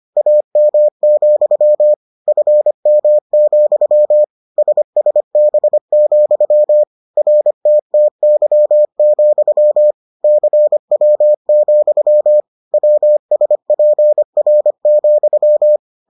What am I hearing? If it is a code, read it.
Morse code, heard as a keyed tone, about 25 wpm, 600 Hz. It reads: AM, FM, SSB, RTTY, CW, WSPR,